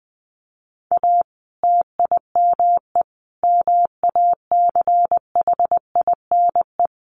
Morse code, heard as a keyed tone, about 20 wpm, 700 Hz. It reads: A TIME MACHINE
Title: A Sound Of Thunder 20wpm